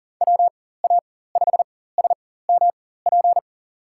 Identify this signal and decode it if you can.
Morse code, heard as a keyed tone, about 40 wpm, spaced as for 19 wpm, 700 Hz. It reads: WA5SMP
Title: USA Call Signs - 6 Characters - 4x Character Spacing - 40wpm